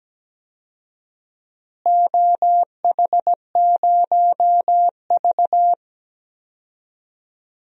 Morse code, heard as a keyed tone, about 17 wpm, 700 Hz. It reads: OH0V
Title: DXCC-100 Call Signs - 4 Characters - Repeated 2x - 17wpm